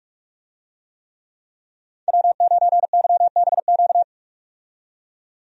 Morse code, heard as a keyed tone, about 48 wpm, 700 Hz. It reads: W9YBQ
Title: USA Call Signs - 5 Characters - 45wpm